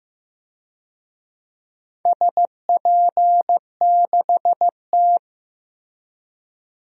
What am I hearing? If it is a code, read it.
Morse code, heard as a keyed tone, about 15 wpm, 700 Hz. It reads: SP6T
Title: DXCC-100 Call Signs - 4 Characters - 15wpm